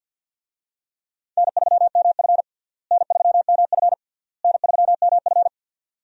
Morse code, heard as a keyed tone, about 51 wpm, 700 Hz. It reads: N3MF N3MF N3MF
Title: USA Call Signs - 4 Characters - Repeated 3x - 50wpm